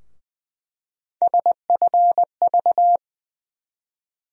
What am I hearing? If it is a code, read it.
Morse code, heard as a keyed tone, about 20 wpm, 700 Hz. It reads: SFV
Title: USA Call Sign Postfixes - 3 Letters 20wpm